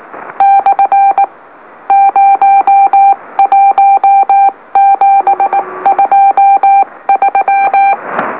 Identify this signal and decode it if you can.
Morse code, heard as a keyed tone, about 19 wpm, spaced as for 17 wpm, 790 Hz. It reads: / 01723
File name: M04- -2345utc-LOLO